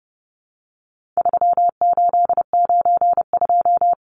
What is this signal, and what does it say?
Morse code, heard as a keyed tone, about 30 wpm, 700 Hz. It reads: 3892
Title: Numbers - 4-Digit 30wpm